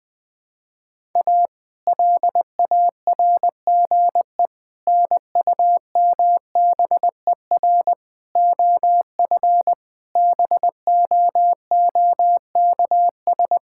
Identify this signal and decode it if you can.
Morse code, heard as a keyed tone, about 21 wpm, 700 Hz. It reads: A LARGE NUMBER OF BOOKS
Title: Sentences with 4-Word Phrases - 20wpm